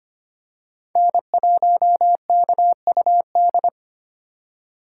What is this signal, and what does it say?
Morse code, heard as a keyed tone, about 25 wpm, 700 Hz. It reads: N1KUD
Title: USA Call Signs - 5 Characters - 4x Character Spacing - 25wpm